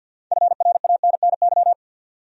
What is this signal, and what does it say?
Morse code, heard as a keyed tone, about 49 wpm, 700 Hz. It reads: FRANNY
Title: USA Common Names - 50wpm